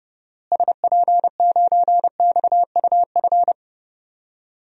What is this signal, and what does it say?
Morse code, heard as a keyed tone, about 30 wpm, 700 Hz. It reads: SP9XUF
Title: DXCC-100 Call Signs - 6 Characters - Repeated 2x - 30wpm